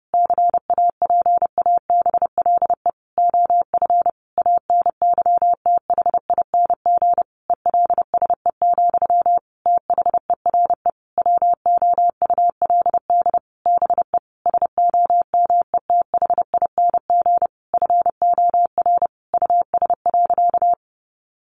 Morse code, heard as a keyed tone, about 30 wpm, 700 Hz. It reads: CAPABLE OF ANYTHING ELSE, THERE WOULD BE SOMETHING FOR US.
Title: Sentences from Top 1200 Words 30wpm